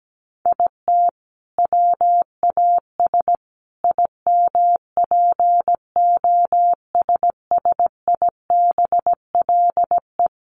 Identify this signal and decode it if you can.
Morse code, heard as a keyed tone, about 17 wpm, 700 Hz. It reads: IT WAS IMPOSSIBLE
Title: Sentences from Top 1100 Words 17wpm